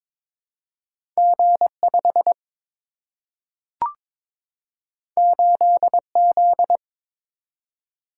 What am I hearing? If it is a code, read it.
Morse code, heard as a keyed tone, about 22 wpm, 700 Hz. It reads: G5 8Z
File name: practice / 22wpm / Two Letter-Number Combinations - 22wpm